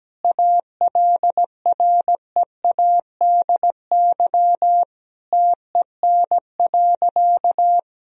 Morse code, heard as a keyed tone, about 17 wpm, 700 Hz. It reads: ALREADY TEN.
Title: Sentences from Top 1000 Words - Review Part 2 - 17wpm